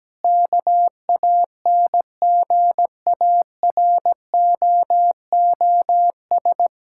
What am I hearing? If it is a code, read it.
Morse code, heard as a keyed tone, about 17 wpm, 700 Hz. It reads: KANGAROOS
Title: Pangram Challenge v2 - 17wpm